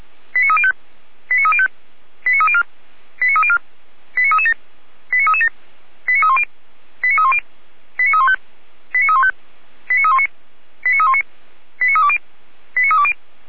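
Five-tone selective call (ZVEI), heard as quick runs of five tones, tones of about 70 ms. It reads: 89305 89305 89304 89304 89337 89337 89310 89310 89315 89315 89319 89319 89320 89320
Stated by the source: Five-tone ZVEI selective call, received with an SDR.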